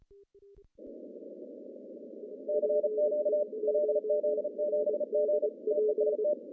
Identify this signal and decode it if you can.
Morse code, heard as a keyed tone, about 34 wpm, 600 Hz. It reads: CQ LZ7G TEST
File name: CW / cqww-cw-k5tr